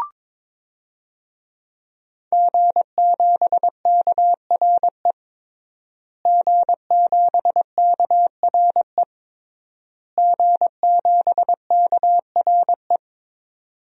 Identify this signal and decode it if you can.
Morse code, heard as a keyed tone, about 22 wpm, 700 Hz. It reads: G7KRE G7KRE G7KRE
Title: DXCC-100 Call Signs - 5 Characters - Repeated 3x - 22wpm